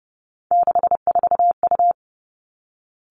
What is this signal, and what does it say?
Morse code, heard as a keyed tone, about 30 wpm, 700 Hz. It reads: 64U
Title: Three Challenging Letter-Number Combinations 30wpm